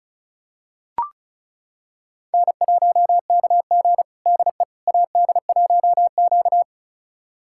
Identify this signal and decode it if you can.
Morse code, heard as a keyed tone, about 36 wpm, 700 Hz. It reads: N1KG DE AD1Q
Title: USA Callsign DE Callsign - 4 Characters - 35wpm